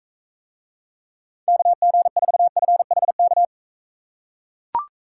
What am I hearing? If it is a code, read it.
Morse code, heard as a keyed tone, about 43 wpm, 700 Hz. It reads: KG4FHK